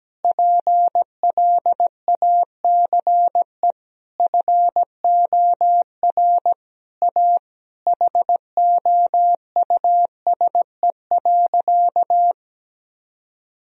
Morse code, heard as a keyed tone, about 17 wpm, 700 Hz. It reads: PLACE FOR A HOUSE.
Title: Sentences from Top 1000 Words - Review Part 3 - 17wpm